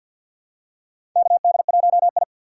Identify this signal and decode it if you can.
Morse code, heard as a keyed tone, about 51 wpm, 700 Hz. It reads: KD1I
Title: USA Call Signs - 4 Characters - 3x Character Spacing - 50wpm